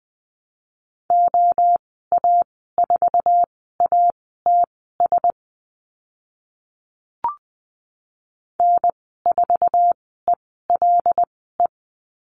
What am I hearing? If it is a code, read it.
Morse code, heard as a keyed tone, about 20 wpm, spaced as for 14 wpm, 700 Hz. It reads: OA4ATS N4ELE
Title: 076 Lesson 13g - Letter L - Call Signs 20wpm